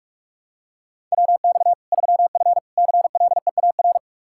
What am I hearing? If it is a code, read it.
Morse code, heard as a keyed tone, about 41 wpm, 700 Hz. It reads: WX 3F CLEAR
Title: K1USN SST QSO Practice 45wpm